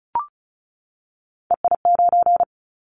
Morse code, heard as a keyed tone, about 36 wpm, 690 Hz. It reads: EI9
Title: Callsign Prefixes for Top 50 DXCC - 35wpm